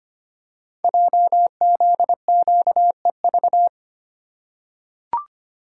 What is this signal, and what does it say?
Morse code, heard as a keyed tone, about 25 wpm, 700 Hz. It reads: JZQEV
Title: Code Groups - 5 Letters - 25wpm